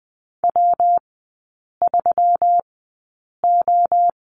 Morse code, heard as a keyed tone, about 20 wpm, spaced as for 8 wpm, 690 Hz. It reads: W3O
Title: 145 Lesson 24e - Letter y - Three Characters 20wpm